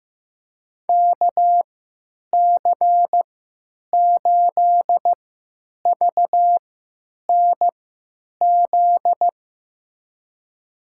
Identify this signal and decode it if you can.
Morse code, heard as a keyed tone, about 15 wpm, spaced as for 9 wpm, 700 Hz. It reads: KC8VNZ
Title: USA Call Signs - 3x Character Spacing - 15wpm